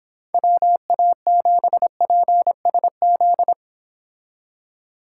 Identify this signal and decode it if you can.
Morse code, heard as a keyed tone, about 31 wpm, 700 Hz. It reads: WA7PSZ